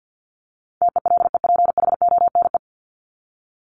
Morse code, heard as a keyed tone, about 50 wpm, 700 Hz. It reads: TELEPHONE
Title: Top 1501-1750 Words 50wpm